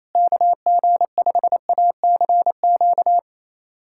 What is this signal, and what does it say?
Morse code, heard as a keyed tone, about 28 wpm, 700 Hz. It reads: KG5ACQ
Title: USA Callsign DE Callsign - 6 Characters - 28wpm